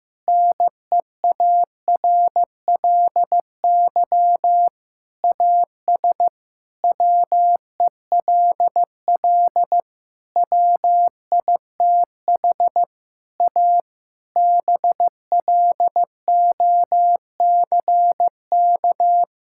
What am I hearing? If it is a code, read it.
Morse code, heard as a keyed tone, about 15 wpm, 700 Hz. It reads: NEARLY AS WELL WITH A BLOCK
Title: Sentences from Top 1600 Words 15wpm